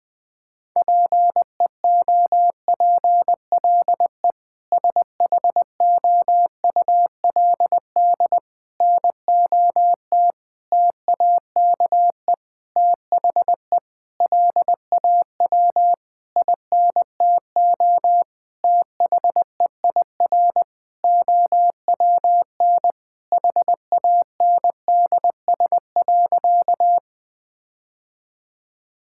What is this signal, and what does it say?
Morse code, heard as a keyed tone, about 20 wpm, 700 Hz. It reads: PEOPLE SHOULD NOT TAKE THE LAW INTO THEIR OWN HANDS.
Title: Sentences with 8-Word Phrases - 20wpm